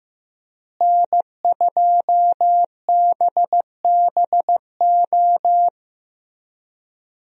Morse code, heard as a keyed tone, about 15 wpm, 700 Hz. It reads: N2BBO
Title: USA Call Signs - 5 Characters - Repeated 3x - 15wpm